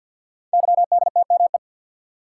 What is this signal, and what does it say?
Morse code, heard as a keyed tone, about 50 wpm, 690 Hz. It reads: YDTME
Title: Code Groups - 5 Letters - 50wpm